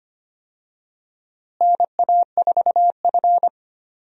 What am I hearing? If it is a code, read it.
Morse code, heard as a keyed tone, about 25 wpm, 700 Hz. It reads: NA4F
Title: USA Call Signs - 4 Characters - Repeated 2x - 25wpm